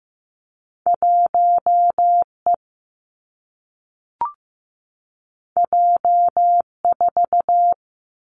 Morse code, heard as a keyed tone, about 15 wpm, 700 Hz. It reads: 1 E J 4
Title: Two Challenging Letter-Number Combinations 15wpm